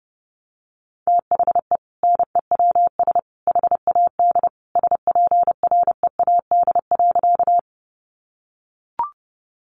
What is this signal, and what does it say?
Morse code, heard as a keyed tone, about 30 wpm, 700 Hz. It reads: THE NEWS HAD SPREAD.
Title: Sentences from Top 1200 Words 30wpm